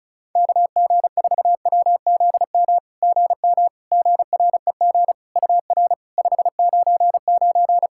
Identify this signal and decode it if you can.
Morse code, heard as a keyed tone, about 34 wpm, 700 Hz. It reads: KG4WZM GM GREG UR 599
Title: SOTA QSO Practice 35wpm